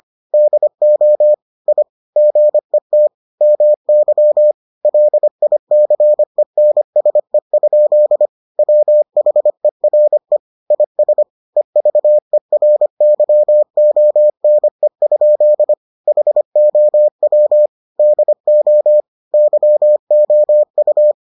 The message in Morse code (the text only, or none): DO I GET MY LICENSE? WHERE IS EVERYONE? HOW DO YOU